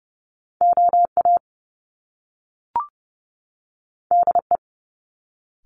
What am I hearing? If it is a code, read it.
Morse code, heard as a keyed tone, about 30 wpm, 700 Hz. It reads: OA DE
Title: Top Two Letter Combinations in Words 30wpm